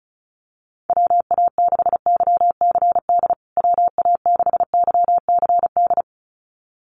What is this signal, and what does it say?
Morse code, heard as a keyed tone, about 34 wpm, 700 Hz. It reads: WA6YCD WA6YCD
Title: USA Call Signs Repeated 2x 35wpm